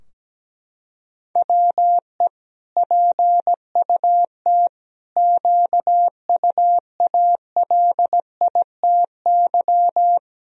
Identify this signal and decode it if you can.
Morse code, heard as a keyed tone, about 17 wpm, 700 Hz. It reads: WE PUT QUALITY
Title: Sentences from Top 1000 Words - Review Part 4 - 17wpm